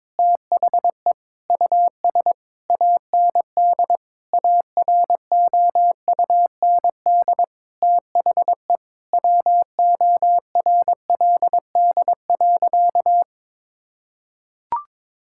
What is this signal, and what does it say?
Morse code, heard as a keyed tone, about 22 wpm, 700 Hz. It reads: THE US AND AROUND THE WORLD.
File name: Sentences with 7-Word Phrases - 22wpm